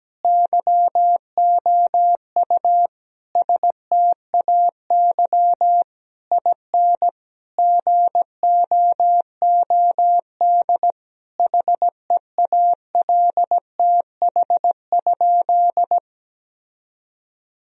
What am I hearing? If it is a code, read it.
Morse code, heard as a keyed tone, about 17 wpm, 700 Hz. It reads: YOU STAY IN GOOD HEALTH?
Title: Sentences from Top 1000 Words - Review Part 3 - 17wpm